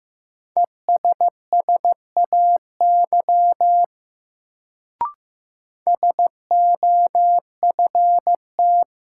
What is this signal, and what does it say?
Morse code, heard as a keyed tone, about 15 wpm, 700 Hz. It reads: ESSAY SOFT
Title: Sentences from Top 1500 Words - New Word Drill Alt - 15wpm